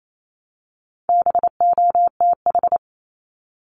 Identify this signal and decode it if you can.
Morse code, heard as a keyed tone, about 28 wpm, 700 Hz. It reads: BOTH
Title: Top 500 Words Repeated 28wpm